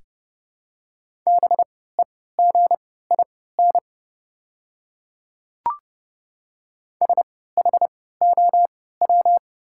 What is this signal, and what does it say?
Morse code, heard as a keyed tone, about 30 wpm, spaced as for 17 wpm, 700 Hz. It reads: BEGIN SHOW
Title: 3x Character Spacing - Top 300 Words - 30wpm